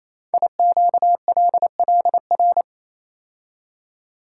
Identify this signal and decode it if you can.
Morse code, heard as a keyed tone, about 29 wpm, 700 Hz. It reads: IQLLR
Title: Code Groups - 5 Letters - 28wpm